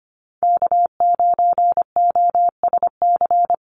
Morse code, heard as a keyed tone, about 25 wpm, 700 Hz. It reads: K9OSC
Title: NA SOTA Chaser Callsigns 25wpm